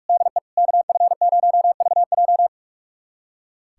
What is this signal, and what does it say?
Morse code, heard as a keyed tone, about 46 wpm, 700 Hz. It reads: DE KF0VJ
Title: USA Callsign DE Callsign - 5 Characters - 45wpm